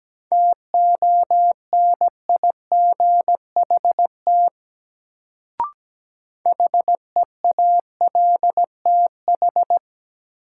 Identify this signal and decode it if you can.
Morse code, heard as a keyed tone, about 17 wpm, 700 Hz. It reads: TONIGHT HEALTH
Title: Sentences from Top 600 Words - New Word Drill - 17wpm